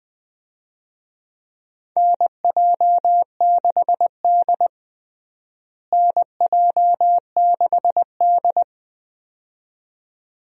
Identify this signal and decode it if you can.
Morse code, heard as a keyed tone, about 21 wpm, 700 Hz. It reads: NJ6D NJ6D
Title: USA Call Signs - 4 Characters - Repeated 2x - 20wpm